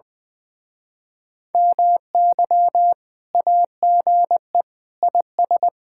Morse code, heard as a keyed tone, about 20 wpm, 700 Hz. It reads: MY AGE IS
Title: Speed-Racing Common QSO Phrases - 20wpm